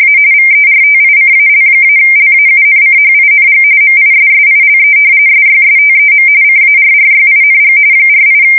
RTTY, about 45.45 baud. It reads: A VERTICAL-LAUNCHING MISSILE SYSTEM CARRYING 48
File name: fsk170 / battleshipnj